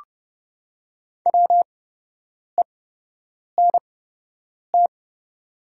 Morse code, heard as a keyed tone, about 30 wpm, spaced as for 8 wpm, 690 Hz. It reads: WENT
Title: 8x Character Spacing - Top 200 Words - 30wpm